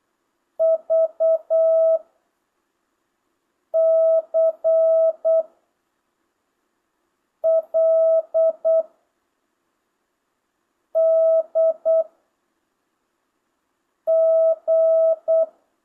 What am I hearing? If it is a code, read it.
Morse code, heard as a keyed tone, about 8 wpm, 650 Hz. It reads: V C L D G
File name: jh8cbh / 2025cw